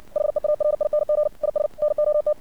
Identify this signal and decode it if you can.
Morse code, heard as a keyed tone, about 64 wpm, 600 Hz. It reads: HEARING IS NOT
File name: archived / CRSnet